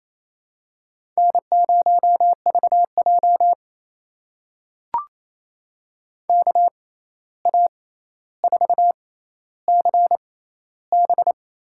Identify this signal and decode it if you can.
Morse code, heard as a keyed tone, about 28 wpm, spaced as for 13 wpm, 700 Hz. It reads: N0VJ KA4CB
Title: USA Call Signs - 6x Character Spacing - 28wpm